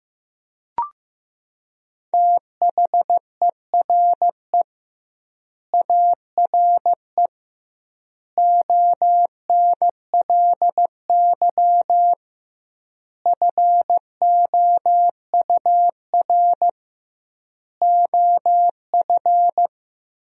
Morse code, heard as a keyed tone, about 16 wpm, 700 Hz. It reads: THERE ARE ONLY FOUR OF
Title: 2x Word-Spacing - Sentences from Top 300 Words - 15wpm